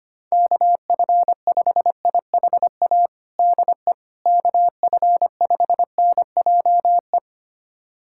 Morse code, heard as a keyed tone, about 25 wpm, 700 Hz. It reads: KF5IHA DE KF5NJE
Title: USA Callsign DE Callsign 25wpm